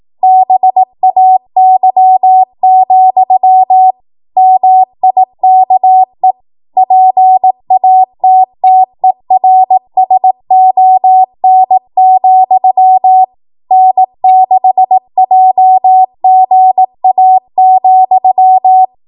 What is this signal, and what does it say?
Morse code, heard as a keyed tone, about 18 wpm, 750 Hz. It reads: BAY, MIKE PATTERSON, N6JGA,
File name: Morse / 18WPM